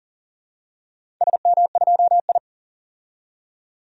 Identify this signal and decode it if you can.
Morse code, heard as a keyed tone, about 40 wpm, 700 Hz. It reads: SM2I
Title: DXCC-100 Call Signs - 4 Characters - 40wpm